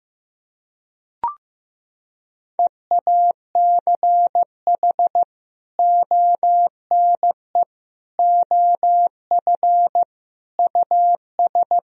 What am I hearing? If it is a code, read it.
Morse code, heard as a keyed tone, about 16 wpm, 700 Hz. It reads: EACH ONE OF US
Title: Sentences from Top 1000 Words - Review Part 2 - 15wpm